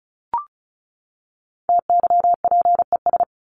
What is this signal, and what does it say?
Morse code, heard as a keyed tone, about 34 wpm, 700 Hz. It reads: TYPES
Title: ES:S Suffix - Top 100 Words with Base Words 35wpm